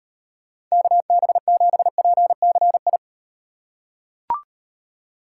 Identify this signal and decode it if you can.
Morse code, heard as a keyed tone, about 38 wpm, 700 Hz. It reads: KB7PCI